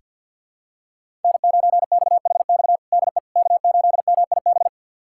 Morse code, heard as a keyed tone, about 47 wpm, 700 Hz. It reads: N9XHX DE K8MIB
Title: USA Callsign DE Callsign - 5 Characters - 50wpm